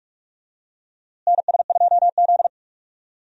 Morse code, heard as a keyed tone, about 46 wpm, 700 Hz. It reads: NS2Z
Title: USA Call Signs - 4 Characters - Repeated 2x - 45wpm